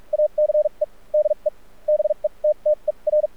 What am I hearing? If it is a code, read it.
Morse code, heard as a keyed tone, about 46 wpm, spaced as for 35 wpm, 600 Hz. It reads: ACE DE BETTER
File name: archived / CRSnet